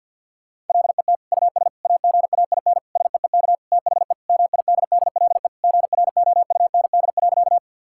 Morse code, heard as a keyed tone, about 45 wpm, 700 Hz. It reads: LET US AGAIN SEEK THE MIDDLE GROUND.
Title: Sentences from Top 1600 Words 50wpm